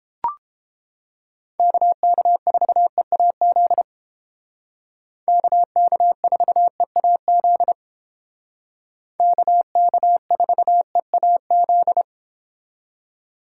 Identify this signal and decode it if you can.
Morse code, heard as a keyed tone, about 30 wpm, 700 Hz. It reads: KK4EAZ KK4EAZ KK4EAZ